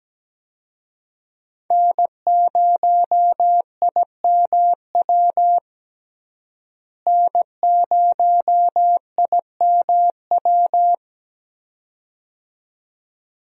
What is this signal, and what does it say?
Morse code, heard as a keyed tone, about 17 wpm, 700 Hz. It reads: N0IMW N0IMW
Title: USA Call Signs - 5 Characters - Repeated 2x - 17wpm